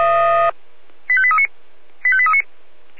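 Five-tone selective call (ZVEI), heard as quick runs of five tones, tones of about 70 ms. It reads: 85839 85839
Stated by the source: ZVEI five-tone selective call received via SDR.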